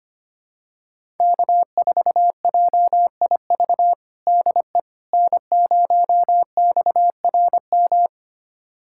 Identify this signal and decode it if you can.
Morse code, heard as a keyed tone, about 25 wpm, 700 Hz. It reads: K4JIV DE N0XRM
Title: USA Callsign DE Callsign - 5 Characters - 25wpm